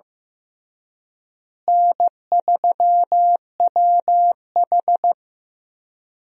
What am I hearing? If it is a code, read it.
Morse code, heard as a keyed tone, about 15 wpm, 700 Hz. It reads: N3WH
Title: USA Call Signs - 4 Characters - Repeated 3x - 15wpm